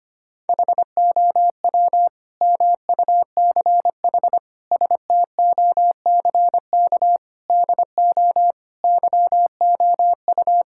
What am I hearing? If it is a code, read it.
Morse code, heard as a keyed tone, about 25 wpm, 700 Hz. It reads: HOW MUCH STOCK DO YOU
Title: Sentences from Top 1000 Words - Review Part 5 - 25wpm